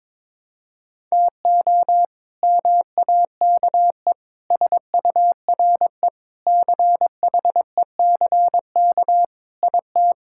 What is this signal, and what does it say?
Morse code, heard as a keyed tone, about 21 wpm, 700 Hz. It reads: TO MAKE SURE CHECK IT
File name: practice / 22wpm / Sentences with 3-Word Phrases - 22wpm